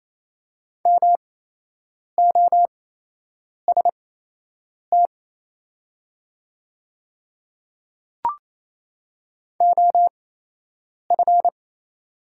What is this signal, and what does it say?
Morse code, heard as a keyed tone, about 28 wpm, spaced as for 8 wpm, 690 Hz. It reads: MOST OF
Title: 8x Character Spacing - Top 100 Words - 28wpm